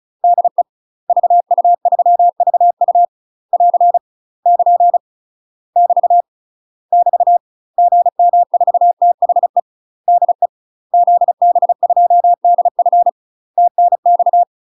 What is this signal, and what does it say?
Morse code, heard as a keyed tone, about 36 wpm, 700 Hz. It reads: DE VU3VU <AR> <KN> <BT> <BT> GM4THE DE ZB2DF TNX
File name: UK MASTER QSOs TRAINING / 35WPM / UK MASTER QSOs 0023